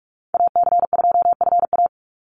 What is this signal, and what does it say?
Morse code, heard as a keyed tone, about 44 wpm, 700 Hz. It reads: AC2FA
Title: Call Signs - 45wpm